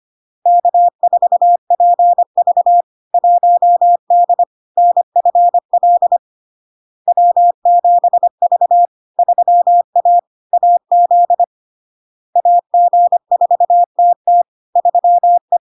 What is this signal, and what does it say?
Morse code, heard as a keyed tone, about 25 wpm, 700 Hz. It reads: K4PV 1D NFL W7V 3A AZ AG4TT 3E
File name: Log-in-CW-25wpm-700-Hz